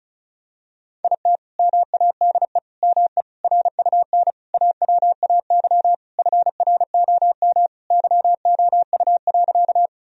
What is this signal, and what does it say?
Morse code, heard as a keyed tone, about 36 wpm, 700 Hz. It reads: IT MADE ME RUN AWAY FROM YOU.
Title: Sentences from Top 1000 Words - Review Part 1 - 35wpm